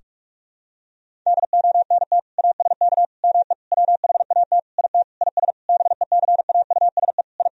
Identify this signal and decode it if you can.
Morse code, heard as a keyed tone, about 46 wpm, 700 Hz. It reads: DONT ASK ME WHAT IT IS BECAUSE I
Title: Sentences from Top 1000 Words - Review Part 3 - 45wpm